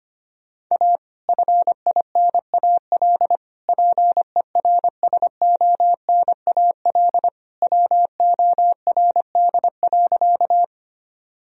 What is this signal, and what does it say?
Morse code, heard as a keyed tone, about 25 wpm, 700 Hz. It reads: A FINAL PERSONAL WORD.
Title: Sentences from Top 1000 Words - Review Part 4 - 25wpm